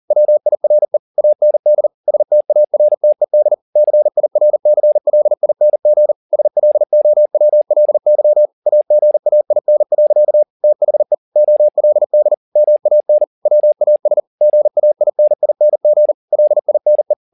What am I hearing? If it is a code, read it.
Morse code, heard as a keyed tone, about 40 wpm, 600 Hz. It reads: WIRE AND STARTED CIRCLING SLOWLY AGAIN. THE OLD MAN WAS GAINING LINE
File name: the old man and the sea 2 40wpm